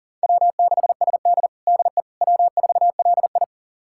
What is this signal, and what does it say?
Morse code, heard as a keyed tone, about 40 wpm, 700 Hz. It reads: W6SD DE W4LI
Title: USA Callsign DE Callsign - 4 Characters - 40wpm